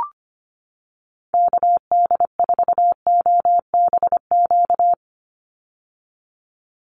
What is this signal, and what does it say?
Morse code, heard as a keyed tone, about 25 wpm, 700 Hz. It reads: KD4OBQ
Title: USA Call Signs 25wpm